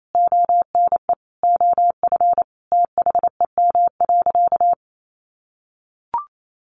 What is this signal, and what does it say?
Morse code, heard as a keyed tone, about 28 wpm, 700 Hz. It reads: ONE OF THEM.
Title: Sentences from Top 1500 Words 28wpm